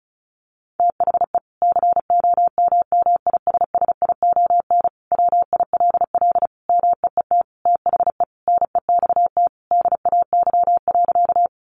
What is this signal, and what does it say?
Morse code, heard as a keyed tone, about 34 wpm, 700 Hz. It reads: THE COMMISSION WILL MEET THE NEXT DAY.
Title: Sentences from Top 1400 Words 35wpm